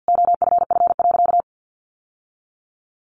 Morse code, heard as a keyed tone, about 50 wpm, 700 Hz. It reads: OFF.
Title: Sentences from Top 1000 Words 50wpm